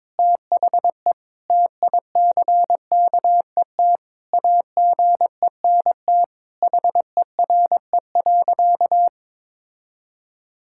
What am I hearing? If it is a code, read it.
Morse code, heard as a keyed tone, about 22 wpm, 700 Hz. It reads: THE TICKET AGENT HERE.
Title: Sentences from Top 1600 Words 22wpm